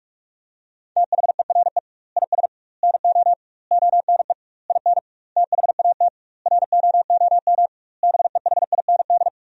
Morse code, heard as a keyed tone, about 46 wpm, 700 Hz. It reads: THERE IS NO ONE IN THAT ROOM BEHIND
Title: 2x Word-Spacing - Sentences from Top 500 Words - 45wpm